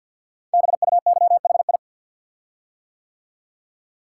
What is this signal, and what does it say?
Morse code, heard as a keyed tone, about 49 wpm, 700 Hz. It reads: BUYHI
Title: Code Groups - 5 Letters - 50wpm